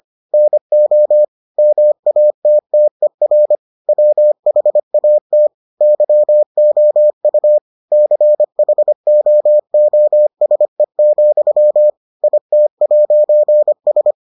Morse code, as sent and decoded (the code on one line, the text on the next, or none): -. --- / -- .- - - . .-. / .-- .... .- - / -.-- --- ..- / -.-. .... --- --- ... --..-- / .. - .----. ...
NO MATTER WHAT YOU CHOOSE, IT'S